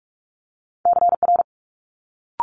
Morse code, heard as a keyed tone, about 46 wpm, 700 Hz. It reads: CR
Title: Multi-letter Phonemes Quick Fire 45wpm